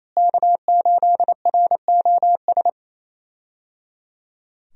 Morse code, heard as a keyed tone, about 28 wpm, 700 Hz. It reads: K8ROS
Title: USA Call Signs - 5 Characters - Repeated 3x - 28wpm